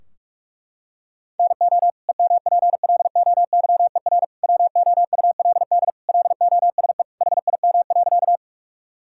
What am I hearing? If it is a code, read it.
Morse code, heard as a keyed tone, about 41 wpm, 700 Hz. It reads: NO EMPLOYER WOULD LOSE HIM.
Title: Sentences from Top 1600 Words 45wpm